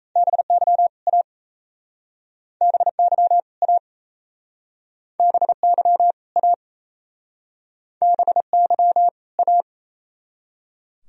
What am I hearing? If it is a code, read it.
Morse code, heard as a keyed tone, about 37 wpm, 690 Hz. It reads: BY A BY A BY A BY A